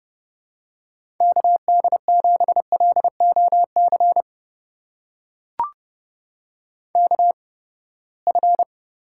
Morse code, heard as a keyed tone, about 31 wpm, spaced as for 21 wpm, 700 Hz. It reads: KD7LOC KF
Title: USA Call Signs - 8x Character Spacing - 30wpm